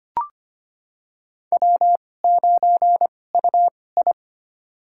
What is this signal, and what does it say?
Morse code, heard as a keyed tone, about 25 wpm, spaced as for 18 wpm, 700 Hz. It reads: W9UI
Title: USA Call Signs - 4 Characters - 2x Character Spacing - 25wpm